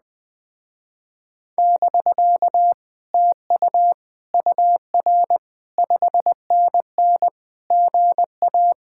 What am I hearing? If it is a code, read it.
Morse code, heard as a keyed tone, about 20 wpm, 700 Hz. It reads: <BK> TU UR 5NN GA
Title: Speed-Racing Common QSO Phrases Encore - 17wpm